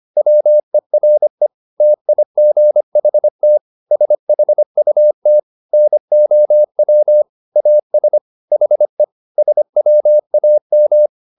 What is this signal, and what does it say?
Morse code, heard as a keyed tone, about 25 wpm, 600 Hz. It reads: WERE TIGHT SHUT NOW AS HE SWAM